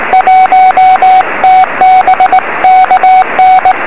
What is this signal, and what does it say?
Morse code, heard as a keyed tone, about 19 wpm, 760 Hz, with radio noise. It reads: 1 TB KN
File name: 0345z dl4dtu 3cm